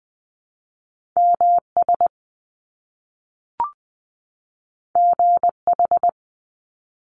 Morse code, heard as a keyed tone, about 20 wpm, 700 Hz. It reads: MS GH
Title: Top Two Letter Combinations in Words 20wpm